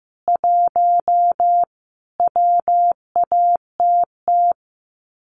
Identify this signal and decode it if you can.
Morse code, heard as a keyed tone, about 15 wpm, 700 Hz. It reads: 1 WATT
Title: Common QSO Phrases 15wpm